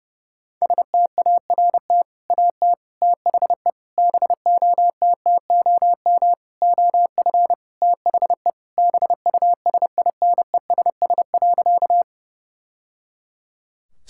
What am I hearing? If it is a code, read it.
Morse code, heard as a keyed tone, about 31 wpm, 700 Hz. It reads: START AT THE BOTTOM OF THE BUSINESS.
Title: Sentences from Top 1000 Words - Review Part 6 - 30wpm